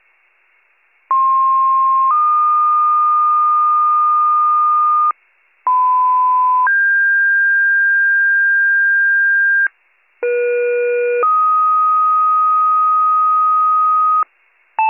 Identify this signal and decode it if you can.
Two-tone page, heard as a one-second tone, then a three-second tone, about 1060 then 1220 Hz, about 1010 then 1600 Hz, about 480 then 1250 Hz. It is heard from 1.1 to 14.2 s.